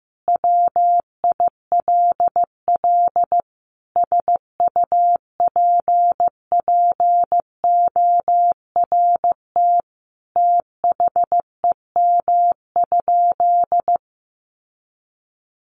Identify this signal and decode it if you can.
Morse code, heard as a keyed tone, about 15 wpm, 700 Hz. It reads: WILL SUPPORT THEM?
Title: Sentences from Top 800 Words 15wpm